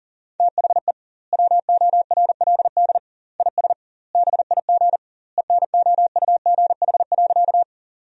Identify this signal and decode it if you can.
Morse code, heard as a keyed tone, about 39 wpm, 700 Hz. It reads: THE WORLD IS BIG ENOUGH.
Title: 2x Word-Spacing - Sentences from Top 400 Words - 40wpm